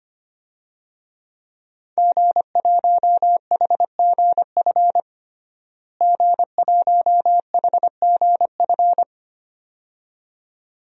Morse code, heard as a keyed tone, about 25 wpm, 700 Hz. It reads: G1HGF G1HGF
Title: DXCC-100 Call Signs - 5 Characters - Repeated 2x - 25wpm